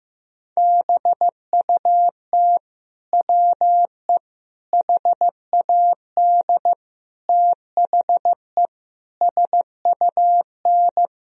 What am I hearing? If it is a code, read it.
Morse code, heard as a keyed tone, about 15 wpm, 700 Hz. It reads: BUT WE HAD THE SUN
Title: Sentences from Top 1000 Words - Review Part 3 - 15wpm